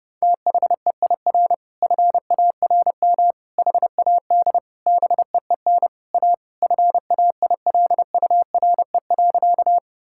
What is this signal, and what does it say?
Morse code, heard as a keyed tone, about 30 wpm, 700 Hz. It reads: THEIR FARM HAD BEEN A FAILURE.
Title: Sentences from Top 1000 Words - Review Part 6 - 30wpm